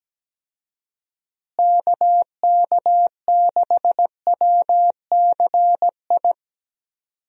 Morse code, heard as a keyed tone, about 17 wpm, 700 Hz. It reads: KK6WCI
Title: USA Call Signs - 6 Characters - Repeated 2x - 17wpm